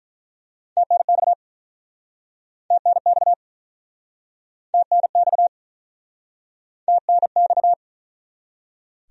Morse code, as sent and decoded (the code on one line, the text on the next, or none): - -. -..- / - -. -..- / - -. -..- / - -. -..-
TNX TNX TNX TNX